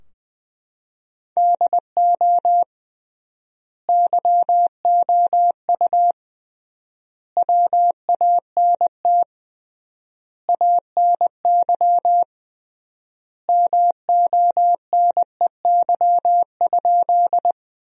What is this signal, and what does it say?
Morse code, heard as a keyed tone, about 20 wpm, 700 Hz. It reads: DO YOU WANT ANY MONEY?
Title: 3x Word-Spacing - Sentences from Top 400 Words - 20wpm